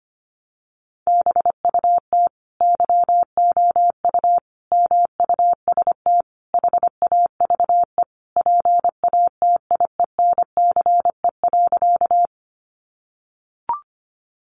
Morse code, heard as a keyed tone, about 25 wpm, 700 Hz. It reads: BUT YOU MUST HAVE PATIENCE.
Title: Sentences from Top 1300 Words 25wpm